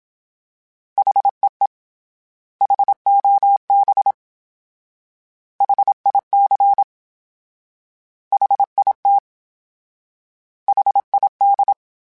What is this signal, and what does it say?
Morse code, heard as a keyed tone, about 27 wpm, 800 Hz. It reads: HEE HOB HIC HIT HID
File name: H Prefixes / H mp3s